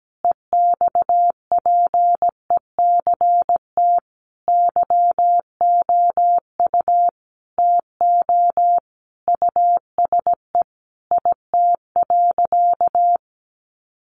Morse code, heard as a keyed tone, about 17 wpm, 700 Hz. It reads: EXPECT YOU TO USE IT.
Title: Sentences from Top 1100 Words 17wpm